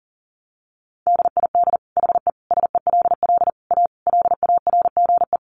Morse code, heard as a keyed tone, about 40 wpm, 700 Hz. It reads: DID HE SELL A LARGE
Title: Sentences from Top 1300 Words 40wpm